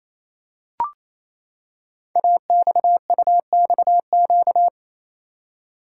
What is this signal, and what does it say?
Morse code, heard as a keyed tone, about 28 wpm, 700 Hz. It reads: AXUXQ
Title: Code Groups - 5 Letters - 28wpm